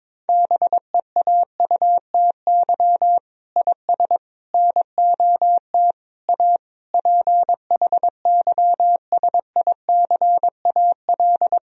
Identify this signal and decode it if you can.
Morse code, heard as a keyed tone, about 22 wpm, 700 Hz. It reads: BEAUTY IS NOT A PHYSICAL
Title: Sentences from Top 1000 Words - Review Part 4 - 22wpm